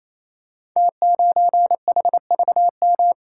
Morse code, heard as a keyed tone, about 27 wpm, 700 Hz. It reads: T9HVM
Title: Code Groups - 5 Letter:Numbers - 28wpm